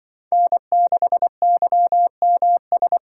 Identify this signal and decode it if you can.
Morse code, heard as a keyed tone, about 25 wpm, 700 Hz. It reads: N6YMS